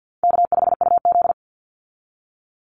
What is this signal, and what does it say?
Morse code, heard as a keyed tone, about 50 wpm, 700 Hz. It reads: K5UZ
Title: Call Signs Repeated 2x - 50wpm